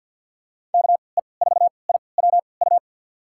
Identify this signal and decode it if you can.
Morse code, heard as a keyed tone, about 52 wpm, spaced as for 28 wpm, 700 Hz. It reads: KE4IWU
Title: USA Call Signs - 6 Characters - 3x Character Spacing - 50wpm